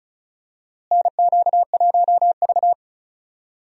Morse code, heard as a keyed tone, about 36 wpm, 700 Hz. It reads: NQ1V
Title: USA Call Signs - 4 Characters - Repeated 3x - 35wpm